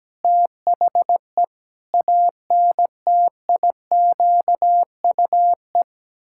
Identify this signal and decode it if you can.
Morse code, heard as a keyed tone, about 17 wpm, 700 Hz. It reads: THE ANTIQUE
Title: Pangram Challenge v2 - 17wpm